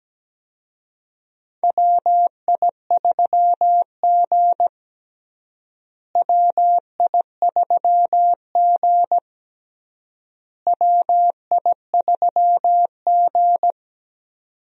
Morse code, heard as a keyed tone, about 17 wpm, 700 Hz. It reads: WI3G WI3G WI3G
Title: USA Call Signs - 4 Characters - Repeated 3x - 17wpm